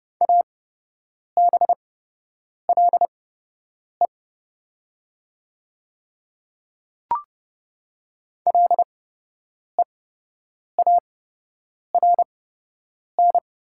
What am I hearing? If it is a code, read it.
Morse code, heard as a keyed tone, about 29 wpm, spaced as for 8 wpm, 700 Hz. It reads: ABLE LEARN
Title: 8x Character Spacing - Top 500 Words - 30wpm